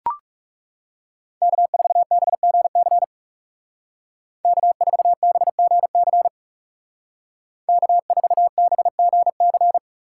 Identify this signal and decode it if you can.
Morse code, heard as a keyed tone, about 36 wpm, 700 Hz. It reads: K4BGC K4BGC K4BGC
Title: Speed Racing - USA 1x3 Call Signs 30wpm